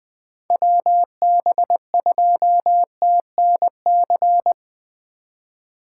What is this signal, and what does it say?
Morse code, heard as a keyed tone, about 20 wpm, 700 Hz. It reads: WB2TNC
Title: USA Call Signs - 6 Characters - 4x Character Spacing - 20wpm